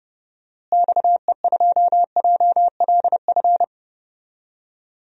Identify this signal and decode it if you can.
Morse code, heard as a keyed tone, about 30 wpm, 700 Hz. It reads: XE2JLF
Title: DXCC-100 Call Signs - 6 Characters - Repeated 3x - 30wpm